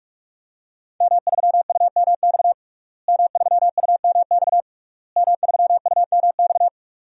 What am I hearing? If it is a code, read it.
Morse code, heard as a keyed tone, about 41 wpm, 700 Hz. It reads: M3UMX M3UMX M3UMX
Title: DXCC-100 Call Signs - 5 Characters - Repeated 3x - 45wpm